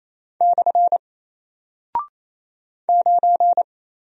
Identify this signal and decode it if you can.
Morse code, heard as a keyed tone, about 27 wpm, 700 Hz. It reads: / 9
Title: Single Character (Letter:Number:Punctuation:Prosign) Alt 28wpm